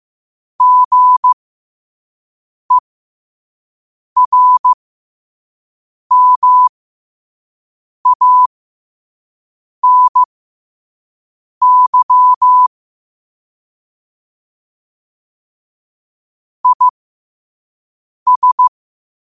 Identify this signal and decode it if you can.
Morse code, heard as a keyed tone, about 15 wpm, spaced as for 5 wpm, 1000 Hz. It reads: GERMANY IS
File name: qotdfiles / 5wpm